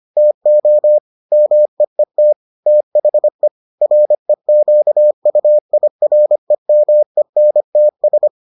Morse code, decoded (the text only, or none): TO MEET THE REQUIREMENTS